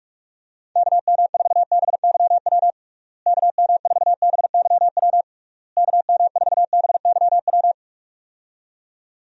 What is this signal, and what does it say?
Morse code, heard as a keyed tone, about 41 wpm, 700 Hz. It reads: KM4BYW KM4BYW KM4BYW
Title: USA Call Signs - 6 Characters - Repeated 3x - 45wpm